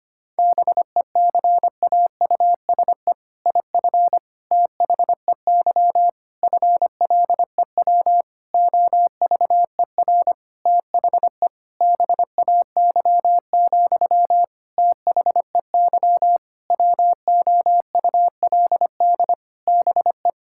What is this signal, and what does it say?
Morse code, heard as a keyed tone, about 25 wpm, 700 Hz. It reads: BECAUSE IF THEY FLEW OVER THE BAY, THEY WOULD BE
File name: practice / 25wpm / All-Age Appropriate Jokes - 25wpm